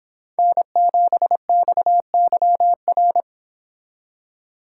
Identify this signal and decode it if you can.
Morse code, heard as a keyed tone, about 26 wpm, 700 Hz. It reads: N7XYR